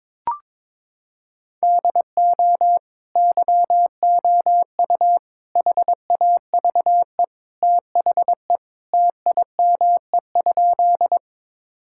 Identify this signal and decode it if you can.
Morse code, heard as a keyed tone, about 22 wpm, 700 Hz. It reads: DO YOU HAVE THE TIME?
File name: practice / 22wpm / Sentences with 2-Word Phrases - 22wpm